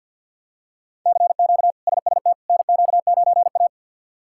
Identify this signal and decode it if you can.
Morse code, heard as a keyed tone, about 50 wpm, 700 Hz. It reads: CQ SST NQ9A
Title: K1USN SST QSO Practice - 50wpm